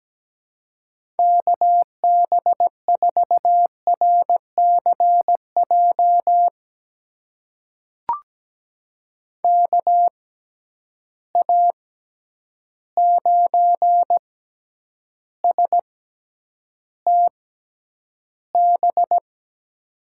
Morse code, heard as a keyed tone, about 17 wpm, spaced as for 9 wpm, 700 Hz. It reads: KB4RCJ KA9STB
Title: USA Call Signs - 6 Characters - 6x Character Spacing - 17wpm